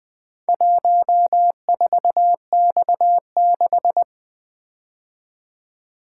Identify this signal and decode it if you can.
Morse code, heard as a keyed tone, about 22 wpm, 700 Hz. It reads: 14X6
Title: Speed-Racing - Four Letter-Number Combinations - 20wpm